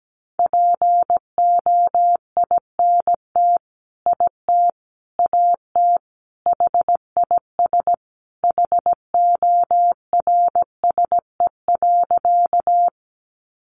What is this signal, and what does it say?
Morse code, heard as a keyed tone, about 17 wpm, 700 Hz. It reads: POINT IT AT HIS HORSE.
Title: Sentences from Top 300 Words 17wpm